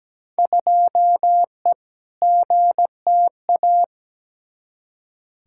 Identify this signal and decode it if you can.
Morse code, heard as a keyed tone, about 17 wpm, 700 Hz. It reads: 2E GTA
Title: Field Day Exchanges Encore - 17wpm